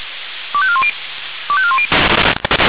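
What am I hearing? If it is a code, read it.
Five-tone selective call (ZVEI), heard as quick runs of five tones, tones of about 70 ms. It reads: 25410 25410
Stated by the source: ZVEI five-tone selective call received by SDR.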